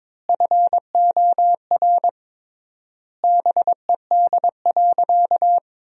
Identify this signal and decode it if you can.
Morse code, heard as a keyed tone, about 21 wpm, 700 Hz. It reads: FOR BED.
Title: 3x Word-Spacing - Sentences from Top 500 Words - 22wpm